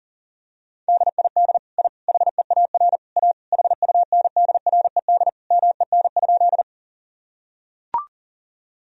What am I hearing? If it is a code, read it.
Morse code, heard as a keyed tone, about 37 wpm, 700 Hz. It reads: DID I HEAR A HUNDRED MEN?
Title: Sentences from Top 1000 Words - Review Part 2 - 40wpm